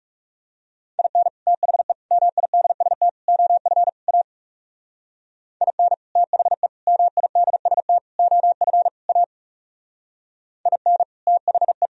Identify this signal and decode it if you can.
Morse code, heard as a keyed tone, about 36 wpm, 700 Hz. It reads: IN THE MIDST OF A IN THE MIDST OF A IN THE
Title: Speed-Racing - Sets of 5 Words 30wpm